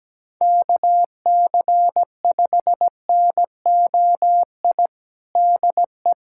Morse code, heard as a keyed tone, about 17 wpm, 700 Hz. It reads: KC5NOI DE
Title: USA Callsign DE Callsign - 6 Characters - 17wpm